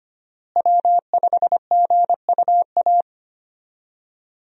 Morse code, heard as a keyed tone, about 25 wpm, 700 Hz. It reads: W5GUA
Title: USA Call Signs - 5 Characters - Repeated 2x - 25wpm